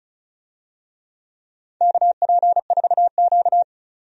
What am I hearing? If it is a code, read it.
Morse code, heard as a keyed tone, about 36 wpm, 700 Hz. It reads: KP4Q
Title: DXCC-100 Call Signs - 4 Characters - 35wpm